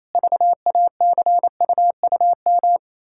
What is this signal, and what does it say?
Morse code, heard as a keyed tone, about 27 wpm, 700 Hz. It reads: VACUUM
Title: Pangram Challenge v2 - 28wpm